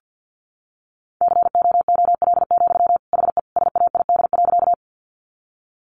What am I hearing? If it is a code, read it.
Morse code, heard as a keyed tone, about 48 wpm, 700 Hz. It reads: COOL, HE SAID.
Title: Sentences from Top 1200 Words 50wpm